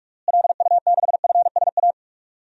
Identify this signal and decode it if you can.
Morse code, heard as a keyed tone, about 46 wpm, 700 Hz. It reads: LU6FSA
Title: DXCC-100 Call Signs - 6 Characters - Repeated 3x - 45wpm